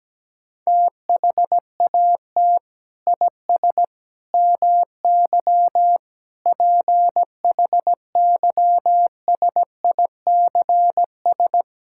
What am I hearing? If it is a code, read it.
Morse code, heard as a keyed tone, about 17 wpm, 700 Hz. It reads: THAT IS MY PHYSICS
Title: Sentences from Top 1000 Words - Review Part 5 - 17wpm